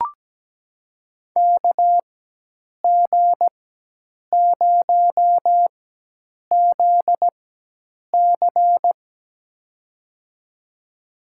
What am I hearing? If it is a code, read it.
Morse code, heard as a keyed tone, about 17 wpm, spaced as for 8 wpm, 700 Hz. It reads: KG0ZC
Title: USA Call Signs - 4x Character Spacing - 17wpm